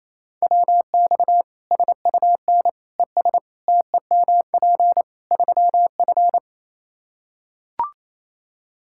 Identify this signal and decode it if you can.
Morse code, heard as a keyed tone, about 28 wpm, 700 Hz. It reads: WX SUN ES TEMP 3F
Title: K1USN SST QSO Practice 28wpm